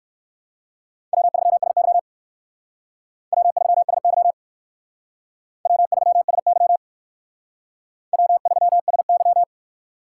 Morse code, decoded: W3SY W3SY W3SY W3SY

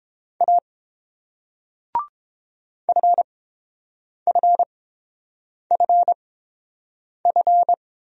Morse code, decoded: A FFFF

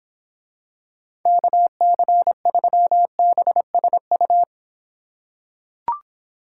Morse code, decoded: KC3BSU

